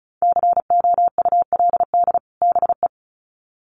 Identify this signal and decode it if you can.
Morse code, heard as a keyed tone, about 34 wpm, 700 Hz. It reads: COULD BE
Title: Sets of 2 Words 35wpm